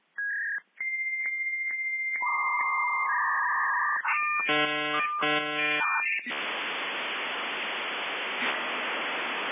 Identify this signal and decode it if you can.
Dial-up modem handshake.